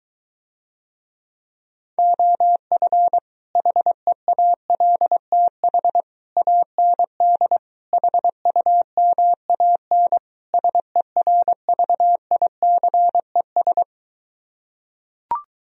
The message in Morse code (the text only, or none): OF HEALTH AND HUMAN SERVICES